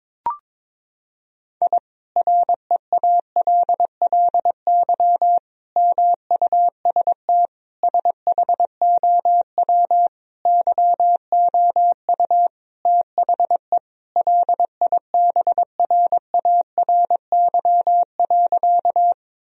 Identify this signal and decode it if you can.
Morse code, heard as a keyed tone, about 22 wpm, 700 Hz. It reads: I REALLY MUST SHOW YOU THE LIBRARY.
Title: Sentences from Top 1000 Words - Review Part 4 - 22wpm